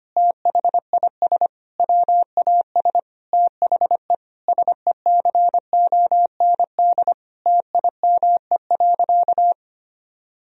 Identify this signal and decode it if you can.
Morse code, heard as a keyed tone, about 25 wpm, 700 Hz. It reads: THIS WAS THE SECOND TIME.
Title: Sentences from Top 1000 Words - Review Part 1 - 25wpm